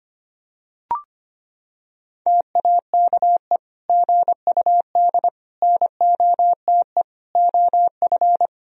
Morse code, heard as a keyed tone, about 25 wpm, 700 Hz. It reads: TAKE GUD NOTE OF
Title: Sentences from Top 500 Words and Abbreviations 25wpm